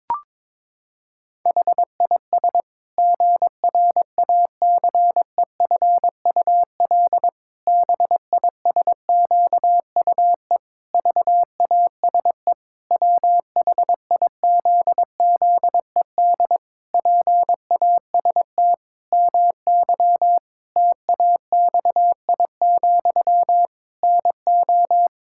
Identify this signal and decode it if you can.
Morse code, heard as a keyed tone, about 22 wpm, 700 Hz. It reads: HIS GRACEFUL BISQUE VASE WHIZZED PAST MY TAXI, NO
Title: Pangram Challenge v2 - 22wpm